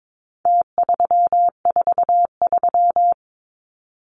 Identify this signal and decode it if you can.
Morse code, heard as a keyed tone, about 22 wpm, 700 Hz. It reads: T343
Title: Numbers - Contest Style - 4-Digit 22wpm